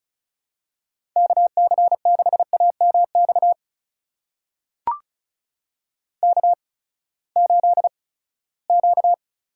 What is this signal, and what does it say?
Morse code, heard as a keyed tone, about 36 wpm, spaced as for 20 wpm, 700 Hz. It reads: KC6AMX K8Q
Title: USA Call Signs - 8x Character Spacing - 35wpm